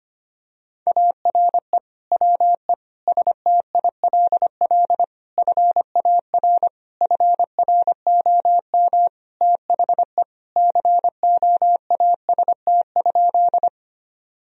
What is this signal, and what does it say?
Morse code, heard as a keyed tone, about 25 wpm, 700 Hz. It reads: ARE WE STILL FAR FROM THE COAST?
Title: Sentences from Top 1000 Words - Review Part 4 - 25wpm